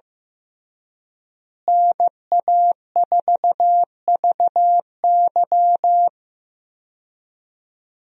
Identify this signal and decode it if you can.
Morse code, heard as a keyed tone, about 15 wpm, 700 Hz. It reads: NA4VY
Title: USA Call Signs - 5 Characters - Repeated 2x - 15wpm